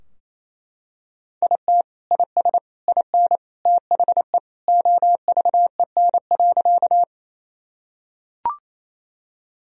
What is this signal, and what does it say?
Morse code, heard as a keyed tone, about 28 wpm, 700 Hz. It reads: IT IS IN THE OVEN.
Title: Sentences from Top 1000 Words - Review Part 4 - 28wpm